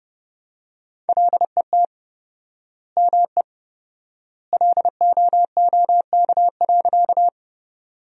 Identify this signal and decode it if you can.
Morse code, heard as a keyed tone, about 32 wpm, 700 Hz. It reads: LET ME LOOK.
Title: 4x Word-Spacing - Sentences from Top 300 Words - 30wpm